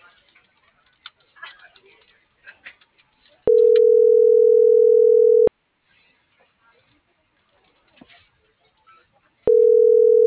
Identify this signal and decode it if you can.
Ringback tone (North America), 440 + 480 Hz.